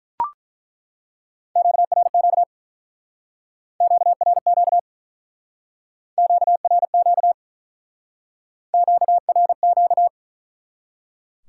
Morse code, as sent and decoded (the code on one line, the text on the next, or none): --.- .-. --.- / --.- .-. --.- / --.- .-. --.- / --.- .-. --.-
QRQ QRQ QRQ QRQ